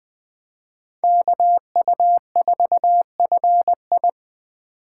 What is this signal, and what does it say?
Morse code, heard as a keyed tone, about 20 wpm, 700 Hz. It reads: KU4FI